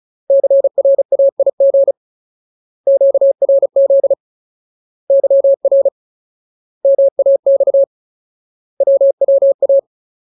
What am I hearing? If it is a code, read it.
Morse code, heard as a keyed tone, about 34 wpm, 550 Hz. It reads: CRAIG QRZ YR MAX WWA
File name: Ultimate Word List / 35WPM Ultimate